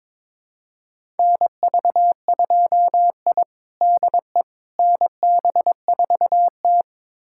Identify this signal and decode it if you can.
Morse code, heard as a keyed tone, about 22 wpm, 700 Hz. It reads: NV2I DE NB4T
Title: USA Callsign DE Callsign - 4 Characters - 22wpm